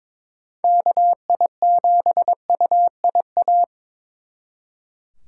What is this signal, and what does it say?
Morse code, heard as a keyed tone, about 26 wpm, 700 Hz. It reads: KI7UIA
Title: Speed Racing - USA Call Signs 22wpm